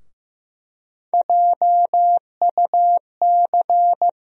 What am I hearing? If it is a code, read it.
Morse code, heard as a keyed tone, about 15 wpm, 700 Hz. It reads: JUC
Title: USA Call Sign Postfixes - 3 Letters 15wpm